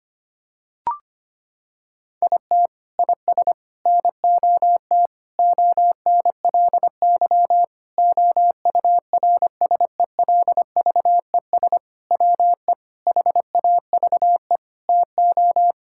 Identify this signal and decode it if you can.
Morse code, heard as a keyed tone, about 25 wpm, 700 Hz. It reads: IT IS NOT ONLY OURSELVES WE HAVE TO
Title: Sentences from Top 1600 Words 25wpm